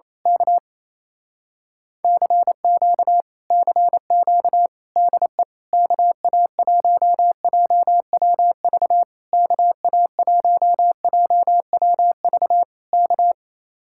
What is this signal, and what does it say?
Morse code, heard as a keyed tone, about 27 wpm, 700 Hz. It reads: K CQ CQ DE KA1JWV KA1JWV K
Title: Speed-Racing Common QSO Phrases Encore - 28wpm